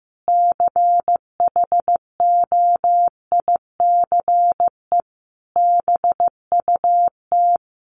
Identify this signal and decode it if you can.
Morse code, heard as a keyed tone, about 15 wpm, 700 Hz. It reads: CHOICE BUT
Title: Sets of 5 Words 15wpm